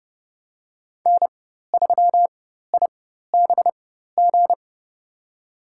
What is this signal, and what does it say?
Morse code, heard as a keyed tone, about 30 wpm, spaced as for 14 wpm, 700 Hz. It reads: N3IBG
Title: USA Call Signs - 5 Characters - 4x Character Spacing - 30wpm